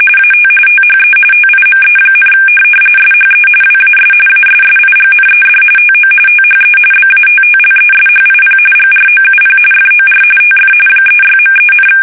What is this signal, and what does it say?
RTTY, about 45.45 baud. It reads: AT THE TIME, SHE WAS ASSIGNED TO THE BATTLESHIP GROUP FOR TF 36.3,
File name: fsk850 / USS-Massachusetts